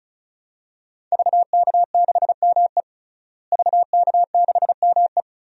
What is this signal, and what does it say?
Morse code, heard as a keyed tone, about 34 wpm, 700 Hz. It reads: VK6ME VK6ME
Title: DXCC-100 Call Signs - 5 Characters - Repeated 2x - 35wpm